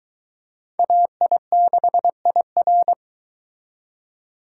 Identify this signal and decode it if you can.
Morse code, heard as a keyed tone, about 23 wpm, 700 Hz. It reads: AI6IR